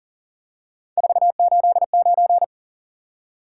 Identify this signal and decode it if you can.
Morse code, heard as a keyed tone, about 39 wpm, 700 Hz. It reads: 489
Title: Top 180 QSO Elements 40wpm